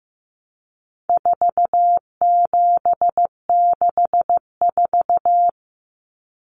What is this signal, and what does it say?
Morse code, heard as a keyed tone, about 15 wpm, 700 Hz. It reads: 4764
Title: Numbers - 4-Digit 15wpm